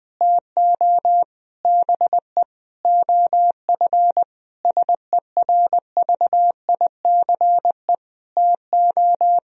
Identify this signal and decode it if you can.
Morse code, heard as a keyed tone, about 20 wpm, 700 Hz. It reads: TO BE OF SERVICE TO
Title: Sentences from Top 1000 Words - Review Part 4 - 20wpm